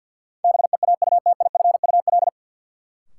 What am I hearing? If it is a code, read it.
Morse code, heard as a keyed tone, about 53 wpm, 700 Hz. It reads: BEAUTIFUL
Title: Sentences from Top 900 Words - New Word Drill - 50wpm